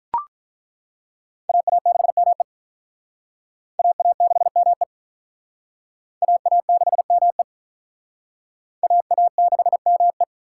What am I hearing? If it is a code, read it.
Morse code, heard as a keyed tone, about 41 wpm, 700 Hz. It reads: AA6ME AA6ME AA6ME AA6ME
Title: Speed Racing - USA 2x2 Call Signs 35wpm